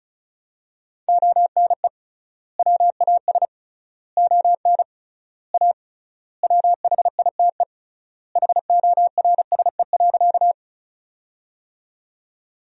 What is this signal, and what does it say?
Morse code, heard as a keyed tone, about 36 wpm, 700 Hz. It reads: ONE WAS ON A WHITE HORSE.
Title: 3x Word-Spacing - Sentences from Top 300 Words - 35wpm